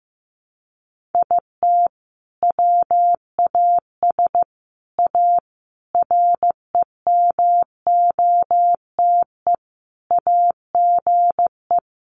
Morse code, heard as a keyed tone, about 15 wpm, 700 Hz. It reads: IT WAS A REMOTE AGE
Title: Sentences from Top 1300 Words 15wpm